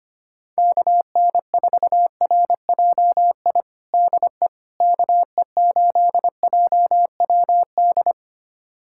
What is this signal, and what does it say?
Morse code, heard as a keyed tone, about 25 wpm, 700 Hz. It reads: KN4RJI DE KE8JWD
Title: USA Callsign DE Callsign - 6 Characters - 25wpm